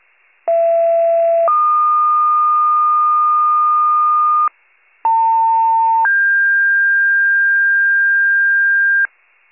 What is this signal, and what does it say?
Two-tone page, heard as a one-second tone, then a three-second tone, about 670 then 1220 Hz, about 900 then 1600 Hz, with noise. It is heard from 0.5 to 9.1 s.